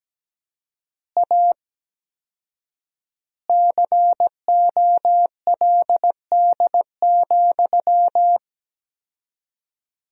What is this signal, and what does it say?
Morse code, heard as a keyed tone, about 17 wpm, 700 Hz. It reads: A COLD,
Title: 4x Word-Spacing - Sentences from Top 500 Words - 17wpm